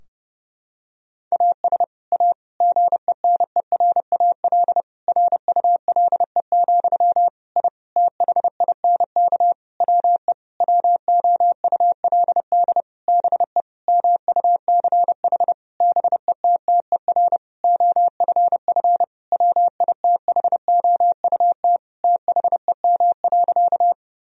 Morse code, heard as a keyed tone, about 32 wpm, 700 Hz. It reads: AS A GENERAL RULE, I THINK WE WOULD BE MUCH BETTER OFF WITHOUT THEM.
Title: Sentences from Top 1000 Words - Review Part 3 - 30wpm